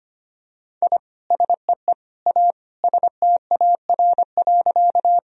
Morse code, heard as a keyed tone, about 25 wpm, 700 Hz. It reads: I SEE A STAR.
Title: Sentences from Top 1000 Words - Review Part 2 - 25wpm